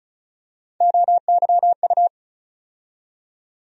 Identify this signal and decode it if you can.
Morse code, heard as a keyed tone, about 34 wpm, 700 Hz. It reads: OYU
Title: USA Call Sign Postfixes - 3 Letters 35wpm